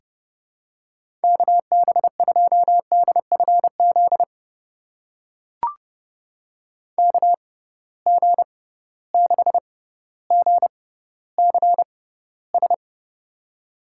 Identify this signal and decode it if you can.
Morse code, heard as a keyed tone, about 32 wpm, spaced as for 16 wpm, 700 Hz. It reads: KB2DFZ KG6GCS
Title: USA Call Signs - 6 Characters - 6x Character Spacing - 30wpm